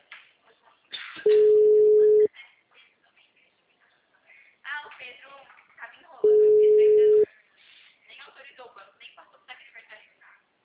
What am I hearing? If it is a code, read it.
Ringback tone (Europe, 425 Hz), with speech.
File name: out- -2004- - - .70754